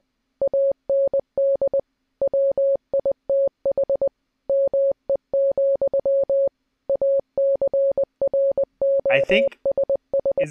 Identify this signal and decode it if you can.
Morse code, heard as a keyed tone, about 20 wpm, 550 Hz, with speech. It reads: AND WITH ME, ACROSS